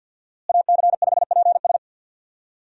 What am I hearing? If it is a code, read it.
Morse code, heard as a keyed tone, about 51 wpm, 700 Hz. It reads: AC5PS
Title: USA Call Signs - 5 Characters - Repeated 3x - 50wpm